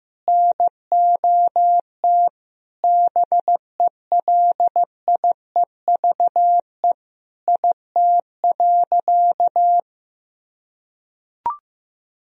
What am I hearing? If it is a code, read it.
Morse code, heard as a keyed tone, about 15 wpm, 700 Hz. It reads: NOT BELIEVE IT.
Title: Sentences from Top 1000 Words - Review Part 1 - 15wpm